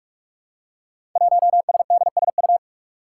Morse code, heard as a keyed tone, about 44 wpm, 700 Hz. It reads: 1SDSU
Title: Five Challenging Letter-Number Combinations 45wpm